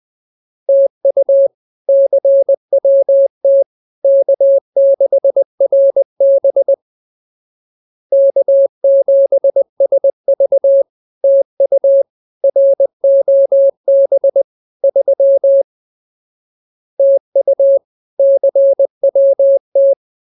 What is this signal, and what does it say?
Morse code, heard as a keyed tone, about 20 wpm, 550 Hz. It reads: TU CWT K6RB K7SV TU ROB 3 TU CWT
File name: CWT-201-20